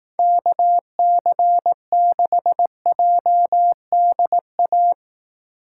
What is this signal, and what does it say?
Morse code, heard as a keyed tone, about 18 wpm, 700 Hz. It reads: KC6JDA
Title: Speed Racing - USA 2x3 Call Signs 15wpm